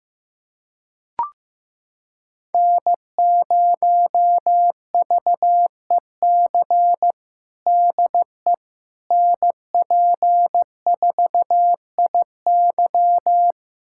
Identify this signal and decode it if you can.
Morse code, heard as a keyed tone, about 15 wpm, 700 Hz. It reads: N0VEC DE NP4IY
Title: USA Callsign DE Callsign - 5 Characters - 15wpm